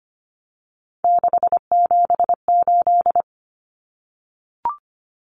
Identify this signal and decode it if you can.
Morse code, heard as a keyed tone, about 25 wpm, 700 Hz. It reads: 678
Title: Numbers - 3-Digit 25wpm